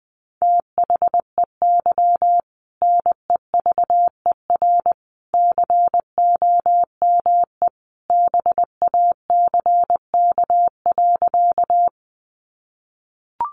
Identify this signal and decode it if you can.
Morse code, heard as a keyed tone, about 20 wpm, 700 Hz. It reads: THEY NEVER COME BACK.
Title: Sentences from Top 300 Words 20wpm